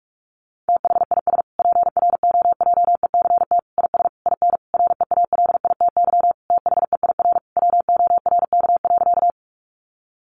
Morse code, heard as a keyed tone, about 44 wpm, 700 Hz. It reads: THIS PROJECT IS IN REALITY THEIR WORK.
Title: Sentences from Top 1400 Words 45wpm